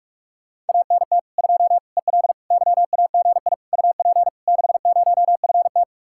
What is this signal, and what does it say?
Morse code, heard as a keyed tone, about 54 wpm, 700 Hz. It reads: ANT 2 EL YAGI UP 60FT
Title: Speed-Racing Common QSO Phrases - 45wpm